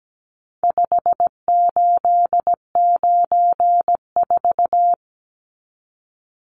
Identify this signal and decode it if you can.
Morse code, heard as a keyed tone, about 17 wpm, 700 Hz. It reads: 5894
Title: Numbers - 4-Digit 17wpm